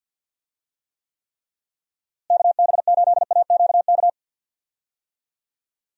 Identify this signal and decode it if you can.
Morse code, heard as a keyed tone, about 51 wpm, 700 Hz. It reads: KB8AQK
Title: USA Call Signs - 6 Characters - Repeated 3x - 50wpm